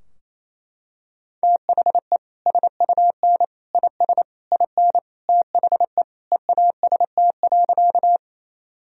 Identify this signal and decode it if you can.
Morse code, heard as a keyed tone, about 28 wpm, 700 Hz. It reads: THE SUN IS IN THE EAST.
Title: Sentences from Top 1000 Words - Review Part 3 - 28wpm